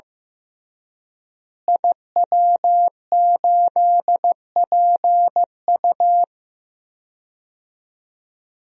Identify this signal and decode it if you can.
Morse code, heard as a keyed tone, about 15 wpm, 700 Hz. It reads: IW8PU
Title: DXCC-100 Call Signs - 5 Characters - Repeated 2x - 15wpm